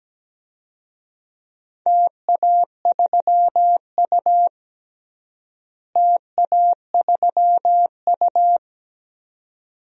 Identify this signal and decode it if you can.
Morse code, heard as a keyed tone, about 17 wpm, 700 Hz. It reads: TA3U TA3U
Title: DXCC-100 Call Signs - 4 Characters - Repeated 2x - 17wpm